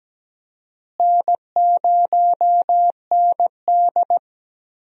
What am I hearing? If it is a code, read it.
Morse code, heard as a keyed tone, about 17 wpm, 700 Hz. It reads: N0ND
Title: USA Call Signs - 4 Characters - Repeated 3x - 17wpm